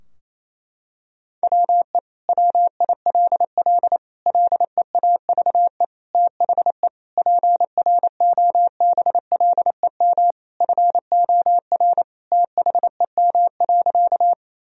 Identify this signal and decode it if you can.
Morse code, heard as a keyed tone, about 28 wpm, 700 Hz. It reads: WE WILL LEAVE THE PROBLEM FOR THEM.
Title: Sentences from Top 1000 Words - Review Part 2 - 28wpm